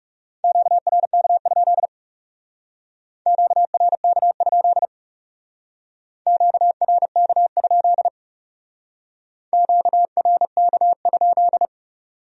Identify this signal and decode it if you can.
Morse code, heard as a keyed tone, about 38 wpm, 700 Hz. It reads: QRK? QRK? QRK? QRK?